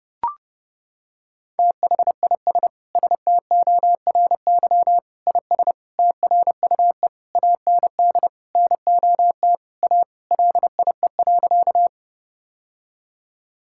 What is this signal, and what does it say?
Morse code, heard as a keyed tone, about 32 wpm, 700 Hz. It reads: THIS STORY IS TRUE AND NOT A LIE.
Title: Sentences from Top 1600 Words 30wpm